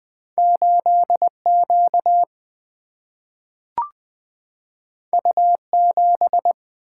Morse code, heard as a keyed tone, about 20 wpm, 700 Hz. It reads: 8 Q U 7
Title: Single-Letter Single-Number 20wpm